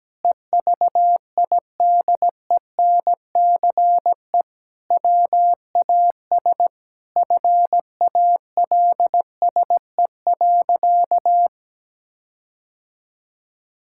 Morse code, heard as a keyed tone, about 17 wpm, 700 Hz. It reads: EVIDENCE WAS FALSE.
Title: Sentences from Top 1600 Words 17wpm